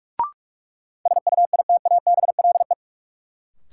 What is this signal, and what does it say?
Morse code, heard as a keyed tone, about 46 wpm, 700 Hz. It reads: SUITABLE
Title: Sentences from Top 1400 Words - New Word Drill Alt - 45wpm